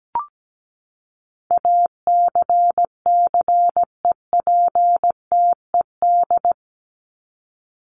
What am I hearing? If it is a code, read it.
Morse code, heard as a keyed tone, about 17 wpm, 700 Hz. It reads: ACCEPTED
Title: ED Suffix - Top 100 Words with Base Words 17wpm